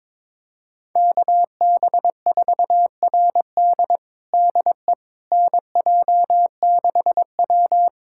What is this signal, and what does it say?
Morse code, heard as a keyed tone, about 22 wpm, 700 Hz. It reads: KB4RD DE NJ6W
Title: USA Callsign DE Callsign 22wpm